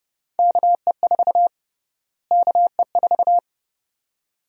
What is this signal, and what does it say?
Morse code, heard as a keyed tone, about 31 wpm, 700 Hz. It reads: KE4 KE4
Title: USA Call Sign Prefixes - Repeated 2x - 30wpm